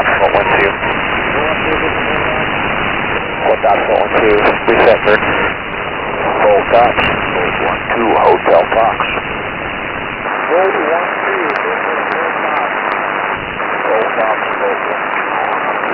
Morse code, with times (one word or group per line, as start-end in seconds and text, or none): none